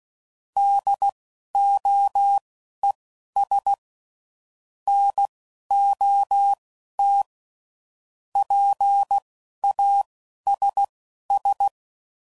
Morse code, heard as a keyed tone, about 16 wpm, spaced as for 11 wpm, 800 Hz. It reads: DOES NOT PASS
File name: CW Code Groups @10WPM / Technical Subjects / SwPwrSup